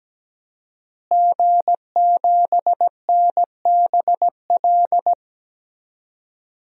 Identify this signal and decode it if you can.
Morse code, heard as a keyed tone, about 17 wpm, 700 Hz. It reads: G7NBL
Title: DXCC-100 Call Signs - 5 Characters - Repeated 2x - 17wpm